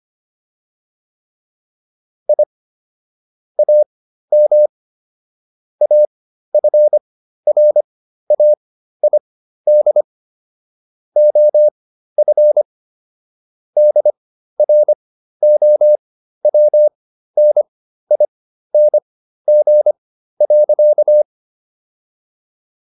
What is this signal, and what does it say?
Morse code, heard as a keyed tone, about 25 wpm, spaced as for 13 wpm, 600 Hz. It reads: I AM AFRAID OF DROWNING.